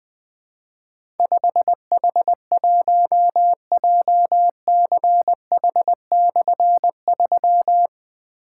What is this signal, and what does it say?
Morse code, heard as a keyed tone, about 20 wpm, 700 Hz. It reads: 5H1JCH/3
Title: Difficult Call Signs 20wpm